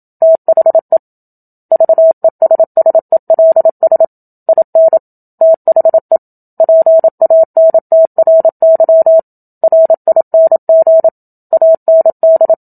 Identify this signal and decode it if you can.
Morse code, heard as a keyed tone, about 28 wpm, 650 Hz. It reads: THE VESSELS IN THE PANTRY RING AND
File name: Book Two / 01-Chapter 1 - 27 WPM